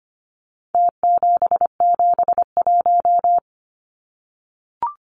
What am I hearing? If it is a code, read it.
Morse code, heard as a keyed tone, about 25 wpm, 700 Hz. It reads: T771
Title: Numbers - Contest Style - 4-Digit 25wpm